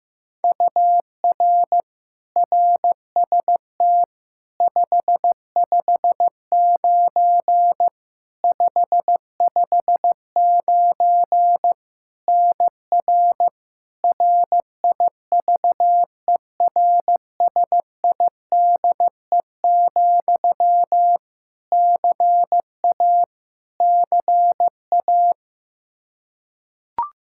Morse code, heard as a keyed tone, about 15 wpm, 700 Hz. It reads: UR RST 559 559 NR RIVERSIDE, CA CA
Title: Practice QSO - Exchange 1 - Beginning - 15wpm